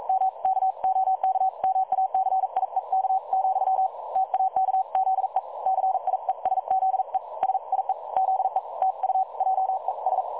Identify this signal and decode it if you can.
Morse code, heard as a keyed tone, about 42 wpm, 760 Hz. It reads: M KOKMACIEK , TAKZE CIESZE SIE ZE TU JES
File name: dlazaawansowanych / nagraniazpasma / websdr recording 3565.0kHz